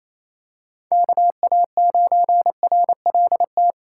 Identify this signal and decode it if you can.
Morse code, heard as a keyed tone, about 28 wpm, 700 Hz. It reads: KA9RLT
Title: USA Call Signs - 3x Character Spacing - 28wpm